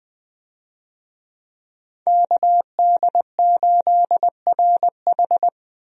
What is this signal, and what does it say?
Morse code, heard as a keyed tone, about 20 wpm, 700 Hz. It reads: KD8RH
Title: USA Call Signs - 5 Characters - Repeated 2x - 20wpm